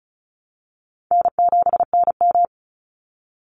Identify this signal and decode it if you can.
Morse code, heard as a keyed tone, about 34 wpm, 700 Hz. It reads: N7NM
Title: Call Signs Repeated 3x - 35wpm